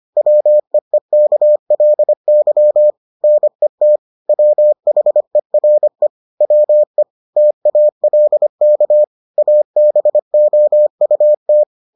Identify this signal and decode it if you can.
Morse code, heard as a keyed tone, about 25 wpm, 600 Hz. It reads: WEEKLY NET WHERE WE TALK ABOUT